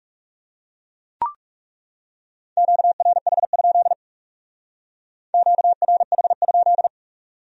Morse code, heard as a keyed tone, about 36 wpm, 700 Hz. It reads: QRH? QRH?
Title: Speed-Racing Q-Codes - 30wpm